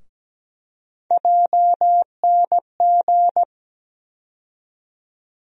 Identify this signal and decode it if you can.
Morse code, heard as a keyed tone, about 17 wpm, 700 Hz. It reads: JNG
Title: USA Call Sign Postfixes - 3 Letters 17wpm